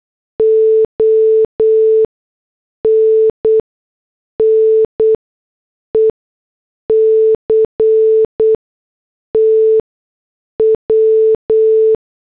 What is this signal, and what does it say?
Morse code, heard as a keyed tone, about 8 wpm, spaced as for 6 wpm, 440 Hz. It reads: ONNECT W
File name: morse / headlines